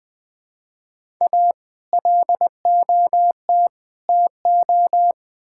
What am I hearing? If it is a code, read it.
Morse code, heard as a keyed tone, about 20 wpm, 700 Hz. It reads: A LOT TO
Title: Build A Phrase - 5-Word Phrases - 20wpm